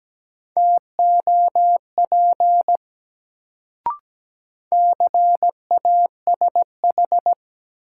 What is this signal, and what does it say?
Morse code, heard as a keyed tone, about 17 wpm, 700 Hz. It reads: TOP CASH
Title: Sentences from Top 700 Words - New Word Drill - 17wpm